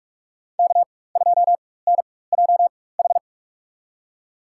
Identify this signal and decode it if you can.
Morse code, heard as a keyed tone, about 46 wpm, spaced as for 21 wpm, 700 Hz. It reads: K2NJH
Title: USA Call Signs - 5 Characters - 4x Character Spacing - 45wpm